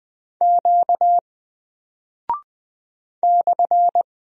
Morse code, heard as a keyed tone, about 20 wpm, 700 Hz. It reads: Q /
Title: Single Character (Letter:Number:Punctuation:Prosign) 20wpm